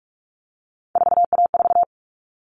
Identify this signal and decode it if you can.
Morse code, heard as a keyed tone, about 44 wpm, 700 Hz. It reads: 4A4
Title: All Cut Numbers - 3-Digit 45wpm